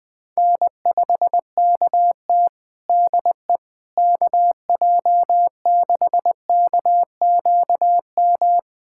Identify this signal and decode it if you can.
Morse code, heard as a keyed tone, about 20 wpm, 700 Hz. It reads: N5KT DE KJ6KQM
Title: Practice QSO - Exchange 4 - Ending - 20wpm